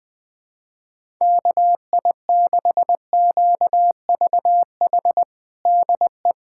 Morse code, heard as a keyed tone, about 20 wpm, 700 Hz. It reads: KI6QVH DE
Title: USA Callsign DE Callsign 20wpm